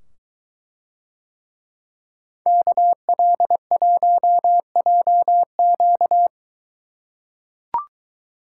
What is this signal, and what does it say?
Morse code, heard as a keyed tone, about 22 wpm, 700 Hz. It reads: KL1JQ